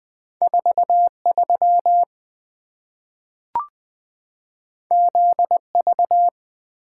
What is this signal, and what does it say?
Morse code, heard as a keyed tone, about 20 wpm, 700 Hz. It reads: 43 ZV
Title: Two Letter-Number Combinations - 20wpm